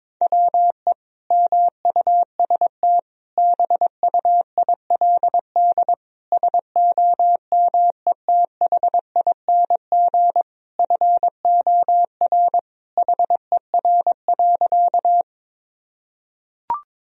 Morse code, heard as a keyed tone, about 22 wpm, 700 Hz. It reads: WE MUST BUILD SOMETHING FOR HER.
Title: Sentences from Top 1000 Words - Review Part 1 - 22wpm